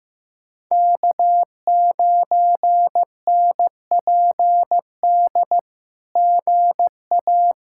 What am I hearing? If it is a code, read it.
Morse code, heard as a keyed tone, about 15 wpm, 700 Hz. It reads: K9NPD GA
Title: SOTA QSO Practice 15wpm